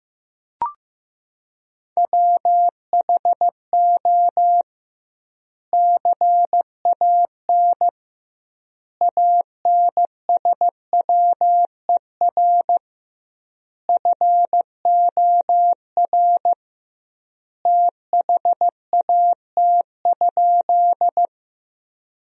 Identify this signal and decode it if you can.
Morse code, heard as a keyed tone, about 15 wpm, 700 Hz. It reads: WHO CAN ANSWER FOR THAT?
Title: 2x Word-Spacing - Sentences from Top 300 Words - 15wpm